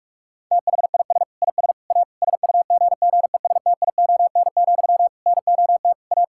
Morse code, heard as a keyed tone, about 44 wpm, 700 Hz. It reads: THIS IS A SUGGESTION, NOT A
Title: Sentences from Top 1600 Words 45wpm